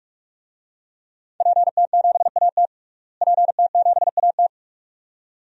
Morse code, heard as a keyed tone, about 43 wpm, 700 Hz. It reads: PT7AT PT7AT
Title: DXCC-100 Call Signs - 5 Characters - Repeated 2x - 45wpm